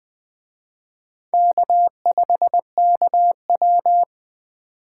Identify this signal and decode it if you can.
Morse code, heard as a keyed tone, about 20 wpm, 700 Hz. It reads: K5KW
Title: Speed Racing - USA 1x2 Call Signs 17wpm